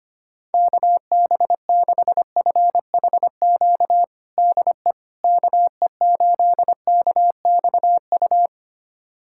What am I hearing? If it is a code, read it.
Morse code, heard as a keyed tone, about 25 wpm, 700 Hz. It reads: KB6FHQ DE KE8KXU
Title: USA Callsign DE Callsign 25wpm